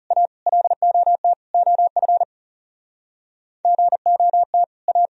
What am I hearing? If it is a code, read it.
Morse code, heard as a keyed tone, about 38 wpm, 700 Hz. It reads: A LOT OF GOT A